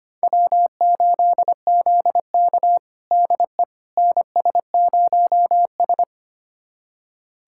Morse code, heard as a keyed tone, about 25 wpm, 700 Hz. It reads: W8ZK DE NS0S
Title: USA Callsign DE Callsign - 4 Characters - 25wpm